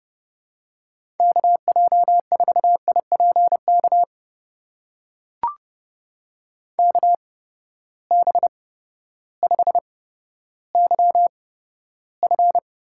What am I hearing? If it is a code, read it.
Morse code, heard as a keyed tone, about 30 wpm, spaced as for 14 wpm, 700 Hz. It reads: KJ4IPK KB5YF
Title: USA Call Signs - 6 Characters - 8x Character Spacing - 30wpm